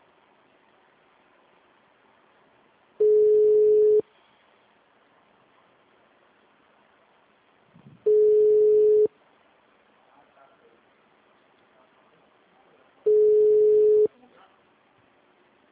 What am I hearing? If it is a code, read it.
Ringback tone (Europe, 425 Hz), with speech.